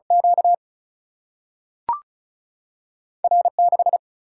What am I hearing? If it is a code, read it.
Morse code, heard as a keyed tone, about 36 wpm, 700 Hz. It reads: Q R 6
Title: Single-Letter Single-Number 35wpm